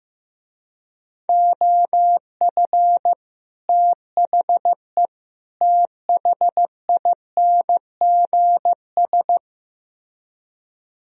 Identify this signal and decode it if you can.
Morse code, heard as a keyed tone, about 15 wpm, 700 Hz. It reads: OF THE THINGS
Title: Speed-Racing - Sets of 3 Words 15wpm